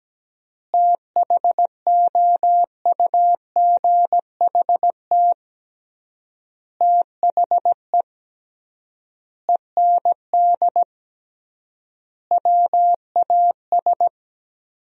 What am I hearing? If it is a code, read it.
Morse code, heard as a keyed tone, about 17 wpm, 700 Hz. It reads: THOUGHT THE END WAS
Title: 3x Word-Spacing - Sentences from Top 300 Words - 17wpm